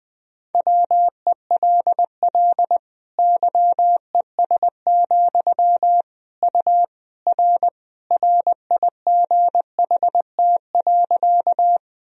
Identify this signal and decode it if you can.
Morse code, heard as a keyed tone, about 21 wpm, 700 Hz. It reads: WELL YES, U R RIGHT.
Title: Sentences from Top 500 Words and Abbreviations 20wpm